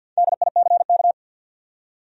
Morse code, heard as a keyed tone, about 50 wpm, 700 Hz. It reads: DICK
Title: USA Common Names - Max 4 Letters - 50wpm